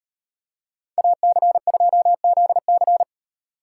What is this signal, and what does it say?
Morse code, heard as a keyed tone, about 33 wpm, 700 Hz. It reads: AC2ZC